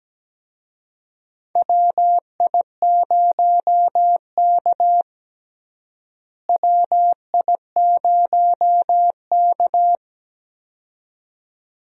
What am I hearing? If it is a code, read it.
Morse code, heard as a keyed tone, about 17 wpm, 700 Hz. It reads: WI0K WI0K
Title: USA Call Signs - 4 Characters - Repeated 2x - 17wpm